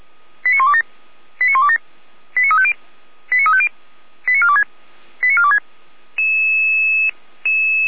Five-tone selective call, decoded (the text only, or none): 89127 89127 89350 89350 89426 89426